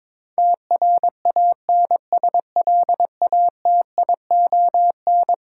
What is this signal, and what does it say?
Morse code, heard as a keyed tone, about 22 wpm, 700 Hz. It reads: TRANSLATION
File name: practice / 22wpm / Eleven Letter Words - 22wpm